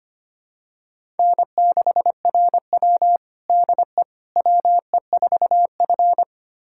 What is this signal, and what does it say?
Morse code, heard as a keyed tone, about 25 wpm, 700 Hz. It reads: N6RW DE WE4F
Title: USA Callsign DE Callsign - 4 Characters - 25wpm